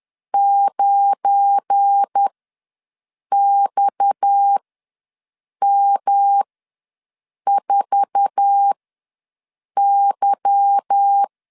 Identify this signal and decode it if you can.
Morse code, heard as a keyed tone, about 10 wpm, spaced as for 6 wpm, 800 Hz. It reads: 9XM4Y
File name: LEZIONI / 10 SIGNS / 10B INT ws 10wpm 4134